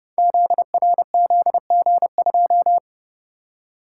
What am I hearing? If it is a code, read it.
Morse code, heard as a keyed tone, about 30 wpm, 700 Hz. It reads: ZRZG2
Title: Code Groups - 5 Letter:Numbers - 30wpm